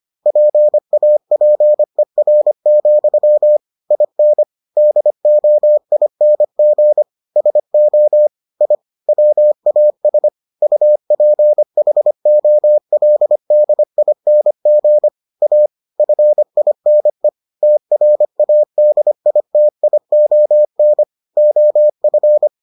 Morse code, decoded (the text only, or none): PAPER, IN DOING SO I WAS UPHOLDING A FINE TRADITION OF